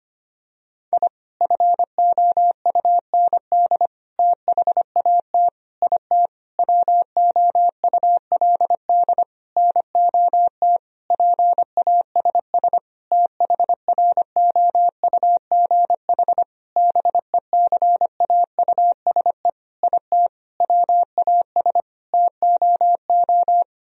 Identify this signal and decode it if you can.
Morse code, heard as a keyed tone, about 25 wpm, 700 Hz. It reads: I FOUND THAT IT WOULD NOT PASS THROUGH BECAUSE IT WAS TOO
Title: Sentences from Top 1000 Words - Review Part 1 - 25wpm